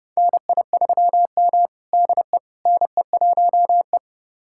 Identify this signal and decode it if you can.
Morse code, heard as a keyed tone, about 30 wpm, 700 Hz. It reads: NI3M DE NE1E
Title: USA Callsign DE Callsign - 4 Characters - 30wpm